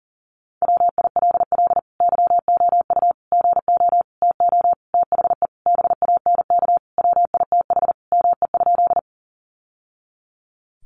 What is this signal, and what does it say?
Morse code, heard as a keyed tone, about 40 wpm, 700 Hz. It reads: WILL YOU GO TO THE BANK WITH ME?
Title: Sentences from Top 900 Words 40wpm